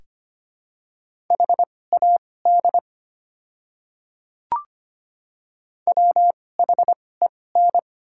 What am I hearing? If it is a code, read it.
Morse code, heard as a keyed tone, about 25 wpm, spaced as for 18 wpm, 690 Hz. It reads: HAD WHEN
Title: 2x Character Spacing - Top 100 Words - 25wpm